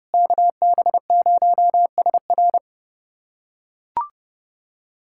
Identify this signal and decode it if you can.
Morse code, heard as a keyed tone, about 32 wpm, 700 Hz. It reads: KB0SR
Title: USA Callsign DE Callsign - 5 Characters - 30wpm